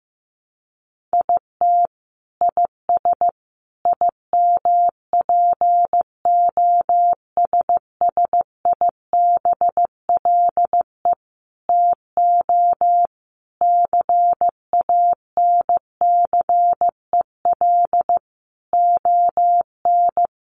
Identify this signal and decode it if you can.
Morse code, heard as a keyed tone, about 15 wpm, 700 Hz. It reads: IT IS IMPOSSIBLE TO CANCEL ON
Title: Sentences from Top 1200 Words 15wpm